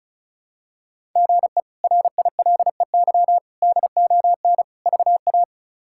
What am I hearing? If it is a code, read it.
Morse code, heard as a keyed tone, about 36 wpm, 700 Hz. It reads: GE RILEY DON VA
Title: K1USN SST QSO Practice - 35wpm